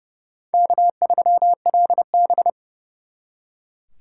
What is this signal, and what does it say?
Morse code, heard as a keyed tone, about 36 wpm, 700 Hz. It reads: K3LB
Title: Speed Racing - USA 1x2 Call Signs 30wpm